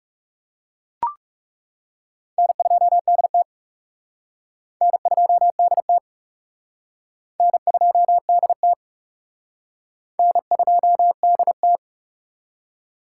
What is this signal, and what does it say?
Morse code, heard as a keyed tone, about 39 wpm, 700 Hz. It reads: N2DT N2DT N2DT N2DT